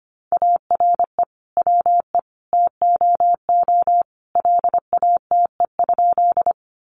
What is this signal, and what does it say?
Morse code, heard as a keyed tone, about 25 wpm, 700 Hz. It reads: ARE WE TOO LATE?
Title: Sentences from Top 300 Words 25wpm